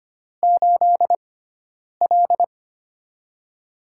Morse code, heard as a keyed tone, about 25 wpm, 700 Hz. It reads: 8 L
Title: USA Call Signs - 4 Characters - 6x Character Spacing - 25wpm